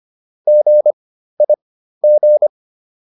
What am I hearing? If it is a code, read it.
Morse code, heard as a keyed tone, about 25 wpm, spaced as for 13 wpm, 600 Hz. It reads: GIG